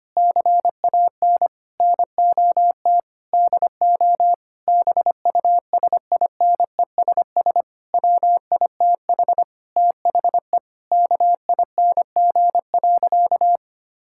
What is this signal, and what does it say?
Morse code, heard as a keyed tone, about 25 wpm, 700 Hz. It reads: CAN NOT DO BUSINESS WITH THE KING.
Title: Sentences from Top 1000 Words - Review Part 6 - 25wpm